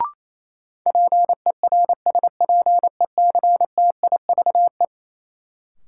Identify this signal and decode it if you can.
Morse code, heard as a keyed tone, about 28 wpm, 700 Hz. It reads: PERSPECTIVE
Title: Sentences from Top 1200 Words - New Word Drill Alt - 28wpm